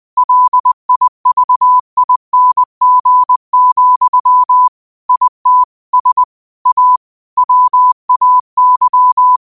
Morse code, heard as a keyed tone, about 20 wpm, 1000 Hz. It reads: LIVING, IT S A WAY
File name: qotdfiles / 20wpm